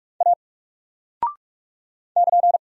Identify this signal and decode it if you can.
Morse code, heard as a keyed tone, about 46 wpm, 700 Hz. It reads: A <KN>
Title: Single Character (Letter:Number:Punctuation:Prosign) 45wpm